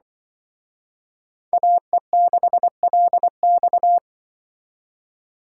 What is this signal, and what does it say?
Morse code, heard as a keyed tone, about 23 wpm, 700 Hz. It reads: AE6LX
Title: Speed Racing - USA 2x2 Call Signs 20wpm